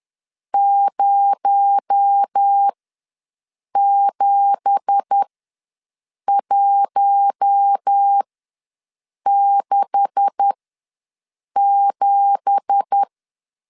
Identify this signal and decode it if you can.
Morse code, heard as a keyed tone, about 10 wpm, spaced as for 6 wpm, 800 Hz. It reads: 07167
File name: LEZIONI / 9 2867 ws 10wpm 4215